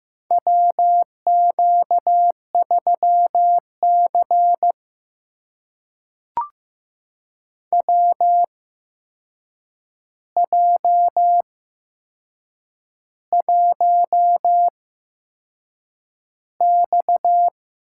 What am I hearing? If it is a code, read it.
Morse code, heard as a keyed tone, about 15 wpm, spaced as for 6 wpm, 700 Hz. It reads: WQ3C WJ1X
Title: USA Call Signs - 4 Characters - 8x Character Spacing - 15wpm